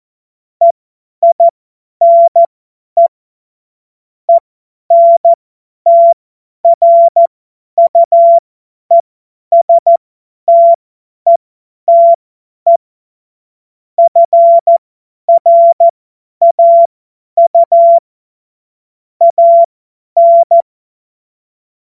Morse code, heard as a keyed tone, about 14 wpm, spaced as for 10 wpm, 680 Hz. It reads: EINE ENTRUESTETE FRAU AN
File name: Morsekurs / Witze / Witz 10. Mai 2022 50 F70